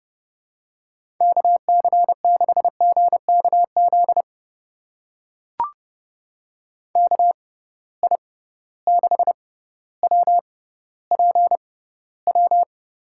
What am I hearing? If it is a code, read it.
Morse code, heard as a keyed tone, about 32 wpm, spaced as for 16 wpm, 700 Hz. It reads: KC6GKZ KI6WPW
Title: USA Call Signs - 6 Characters - 6x Character Spacing - 30wpm